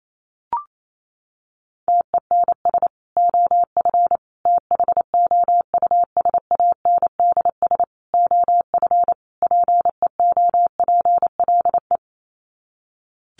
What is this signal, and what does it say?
Morse code, heard as a keyed tone, about 28 wpm, 700 Hz. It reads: TENS OF THOUSANDS OF PEOPLE
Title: Sets of 5 Words Encore2 28wpm